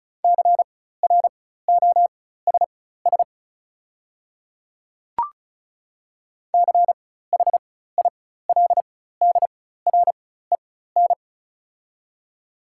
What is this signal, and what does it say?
Morse code, heard as a keyed tone, about 34 wpm, spaced as for 16 wpm, 700 Hz. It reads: CROSS CHILDREN
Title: 4x Character Spacing - Top 500 Words - 35wpm